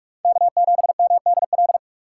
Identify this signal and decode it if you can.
Morse code, heard as a keyed tone, about 46 wpm, 700 Hz. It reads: K7MDL
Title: USA Call Signs - 5 Characters - 4x Character Spacing - 45wpm